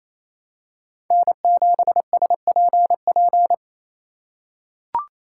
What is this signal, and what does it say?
Morse code, heard as a keyed tone, about 28 wpm, 700 Hz. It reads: N7SPP
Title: USA Call Signs - 5 Characters - 2x Character Spacing - 28wpm